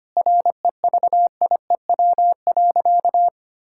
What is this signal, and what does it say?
Morse code, heard as a keyed tone, about 25 wpm, 700 Hz. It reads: REVIEW.
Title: Sentences from Top 1000 Words - Review Part 6 - 25wpm